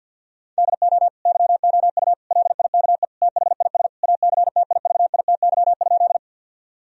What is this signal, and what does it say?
Morse code, heard as a keyed tone, about 50 wpm, 700 Hz. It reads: DO YOU LIKE THIS ACTIVITY?
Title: Sentences from Top 1000 Words - Review Part 4 - 50wpm